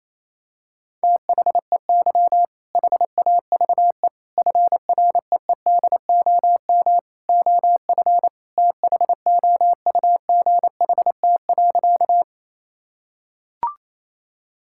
Morse code, heard as a keyed tone, about 27 wpm, 700 Hz. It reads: THEY HAVE FREEDOM OF THOUGHT.
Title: Sentences from Top 1000 Words - Review Part 5 - 28wpm